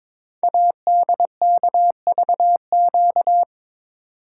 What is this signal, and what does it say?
Morse code, heard as a keyed tone, about 22 wpm, 700 Hz. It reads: ADKVQ
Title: Code Groups - 5 Letters - 22wpm